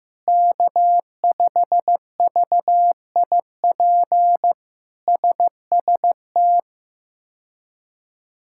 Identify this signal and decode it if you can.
Morse code, heard as a keyed tone, about 15 wpm, 700 Hz. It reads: K5VIP SST
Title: K1USN SST QSO Practice - 15wpm